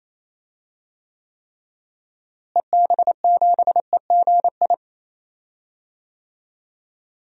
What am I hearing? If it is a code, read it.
Morse code, heard as a keyed tone, about 28 wpm, 700 Hz. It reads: EB7EGI
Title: DXCC-100 Call Signs - 6 Characters - Repeated 3x - 28wpm